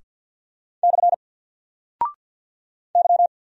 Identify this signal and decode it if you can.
Morse code, heard as a keyed tone, about 50 wpm, 700 Hz. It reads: / Y
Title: Single Character (Letter:Number:Punctuation:Prosign) 50wpm